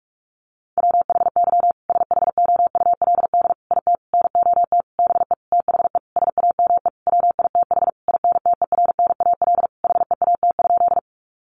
Morse code, heard as a keyed tone, about 44 wpm, 700 Hz. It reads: WHY SHOULD IT NOT BE THE SAME WITH INTERNAL HEAT?
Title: Sentences from Top 1400 Words 45wpm